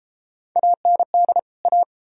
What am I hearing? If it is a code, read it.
Morse code, heard as a keyed tone, about 33 wpm, 700 Hz. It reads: AND A